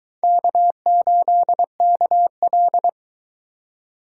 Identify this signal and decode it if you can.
Morse code, heard as a keyed tone, about 23 wpm, 700 Hz. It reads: K8KL